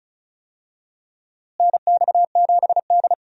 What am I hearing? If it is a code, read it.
Morse code, heard as a keyed tone, about 34 wpm, 700 Hz. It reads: NX7D
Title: USA Call Signs - 4 Characters - Repeated 3x - 35wpm